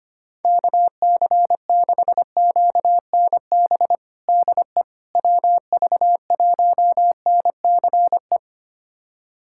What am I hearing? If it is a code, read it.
Morse code, heard as a keyed tone, about 25 wpm, 700 Hz. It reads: KC6QNB DE WV1NCE
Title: USA Callsign DE Callsign - 6 Characters - 25wpm